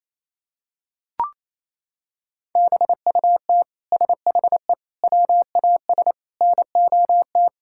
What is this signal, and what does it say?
Morse code, heard as a keyed tone, about 28 wpm, 700 Hz. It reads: BUT SHE WAS NOT
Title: Sentences from Top 1000 Words - Review Part 5 - 28wpm